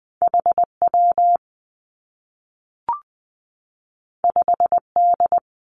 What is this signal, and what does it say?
Morse code, heard as a keyed tone, about 20 wpm, 700 Hz. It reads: H W 5 D
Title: Two Challenging Letter-Number Combinations 20wpm